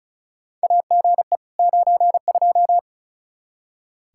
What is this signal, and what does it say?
Morse code, heard as a keyed tone, about 36 wpm, 700 Hz. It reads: AGE 92
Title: Practice QSO - Exchange 3 - More Info - 35wpm